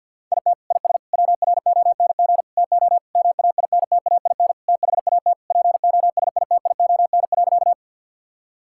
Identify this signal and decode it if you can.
Morse code, heard as a keyed tone, about 50 wpm, 700 Hz. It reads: IT IS WRONG TO MAINTAIN THAT POSITION.
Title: Sentences from Top 1000 Words - Review Part 5 - 50wpm